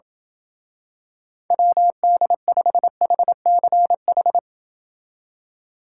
Morse code, heard as a keyed tone, about 28 wpm, 700 Hz. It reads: WD5HCH